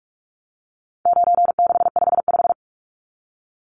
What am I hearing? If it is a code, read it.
Morse code, heard as a keyed tone, about 44 wpm, 700 Hz. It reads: 9655
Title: Numbers - 4-Digit 45wpm